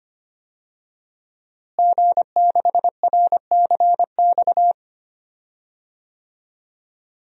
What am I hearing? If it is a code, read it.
Morse code, heard as a keyed tone, about 25 wpm, 700 Hz. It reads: G6RCX
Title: DXCC-100 Call Signs - 5 Characters - Repeated 3x - 25wpm